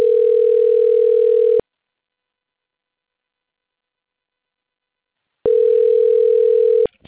Ringback tone (North America), 440 + 480 Hz.
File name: out- - - - - .40264